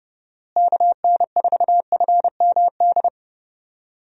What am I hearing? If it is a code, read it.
Morse code, heard as a keyed tone, about 31 wpm, 700 Hz. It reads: KN4FMD
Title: USA Call Signs - 6 Characters - Repeated 3x - 30wpm